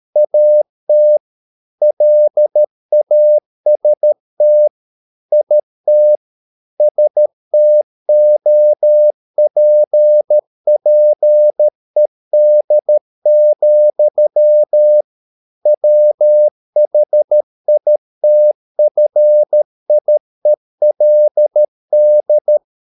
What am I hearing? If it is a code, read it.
Morse code, heard as a keyed tone, about 13 wpm, 600 Hz. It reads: AT LAST IT STOPPED, WHITFIELD